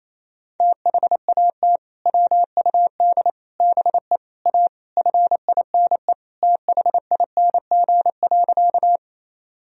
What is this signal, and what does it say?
Morse code, heard as a keyed tone, about 28 wpm, 700 Hz. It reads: THAT WUD BE A FINE THING.
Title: Sentences from Top 500 Words and Abbreviations 28wpm